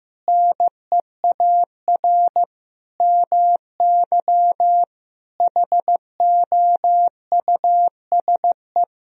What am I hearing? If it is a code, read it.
Morse code, heard as a keyed tone, about 15 wpm, 700 Hz. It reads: NEAR MY HOUSE
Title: Sentences from Top 1000 Words - Review Part 1 - 15wpm